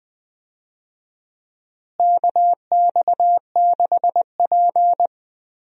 Morse code, heard as a keyed tone, about 20 wpm, 700 Hz. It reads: KX6P
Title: USA Call Signs - 4 Characters - Repeated 3x - 20wpm